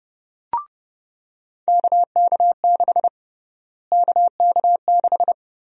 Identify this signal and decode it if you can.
Morse code, heard as a keyed tone, about 30 wpm, 700 Hz. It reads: KK6 KK6
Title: USA Call Sign Prefixes - Repeated 2x - 30wpm